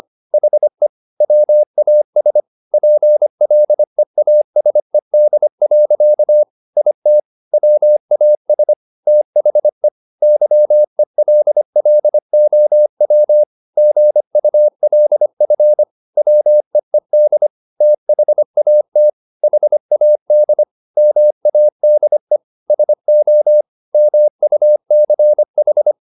Morse code, heard as a keyed tone, about 25 wpm, 600 Hz. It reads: HE WAS PLEASED. IT WAS THE YELLOW GULF WEED THAT HAD MADE SO MUCH
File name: the old man and the sea part 1 25wpm